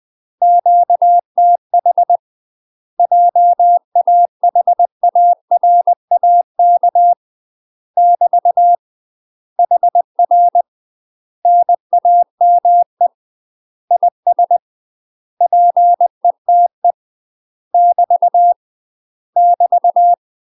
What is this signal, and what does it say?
Morse code, heard as a keyed tone, about 20 wpm, 700 Hz. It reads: QTH JAHARAK <BT> HR NAME IS PETE <BT> <BT>
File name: UK MASTER QSOs TRAINING / 20WPM / UK MASTER QSOs 0025